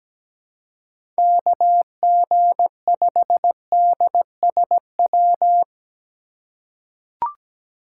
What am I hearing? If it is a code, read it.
Morse code, heard as a keyed tone, about 17 wpm, 700 Hz. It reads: KG5DSW
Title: USA Call Signs - 4x Character Spacing - 17wpm